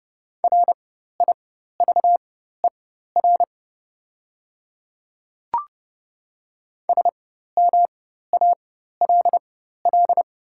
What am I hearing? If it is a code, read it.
Morse code, heard as a keyed tone, about 30 wpm, spaced as for 14 wpm, 690 Hz. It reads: RIVER SMALL
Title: 4x Character Spacing - Top 500 Words - 30wpm